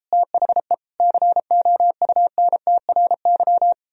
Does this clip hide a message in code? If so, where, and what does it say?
Yes: Morse code from 0.1 to 3.7 s, heard as a keyed tone, about 33 wpm, 700 Hz. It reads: THE COUNTRY